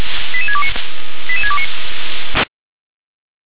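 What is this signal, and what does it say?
Five-tone selective call (ZVEI), heard as quick runs of five tones, tones of about 70 ms. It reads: 88629 88629
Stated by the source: ZVEI five-tone selective call received via SDR.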